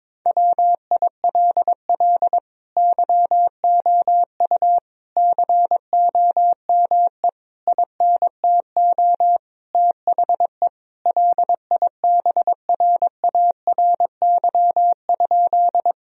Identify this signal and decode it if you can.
Morse code, heard as a keyed tone, about 22 wpm, 700 Hz. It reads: WILL YOU COME INTO THE LIBRARY?
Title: Sentences from Top 1000 Words - Review Part 4 - 22wpm